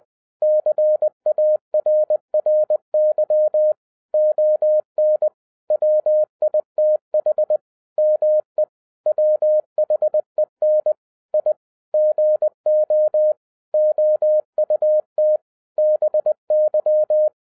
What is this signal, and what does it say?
Morse code, heard as a keyed tone, about 20 wpm, 610 Hz. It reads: CARRY ON WITH ME WHEN I GO OUT BY
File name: 2012-01-mp3 / n05-CAME BACK TO HAM RADIO AFTER 30 YEARS OF INTERVAL-20wpm